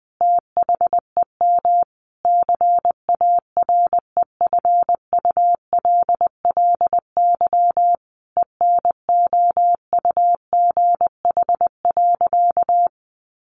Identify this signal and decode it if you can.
Morse code, heard as a keyed tone, about 20 wpm, 700 Hz. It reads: THEM CAREFULLY ENOUGH.
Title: Sentences from Top 1300 Words 20wpm